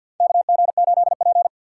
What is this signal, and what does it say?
Morse code, heard as a keyed tone, about 47 wpm, 700 Hz. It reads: KG8P
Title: USA Call Signs - 4 Characters - 3x Character Spacing - 50wpm